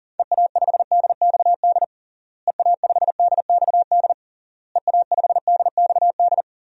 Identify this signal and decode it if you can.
Morse code, heard as a keyed tone, about 40 wpm, 700 Hz. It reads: EA5DXD EA5DXD EA5DXD
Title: DXCC-100 Call Signs - 6 Characters - Repeated 3x - 40wpm